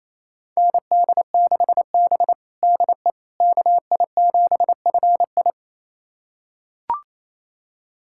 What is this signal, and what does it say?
Morse code, heard as a keyed tone, about 28 wpm, 700 Hz. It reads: ND6B DE KI7FI
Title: USA Callsign DE Callsign 28wpm